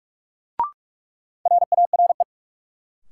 Morse code, heard as a keyed tone, about 46 wpm, 700 Hz. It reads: RARE
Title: Sentences from Top 1100 Words - New Word Drill - 45wpm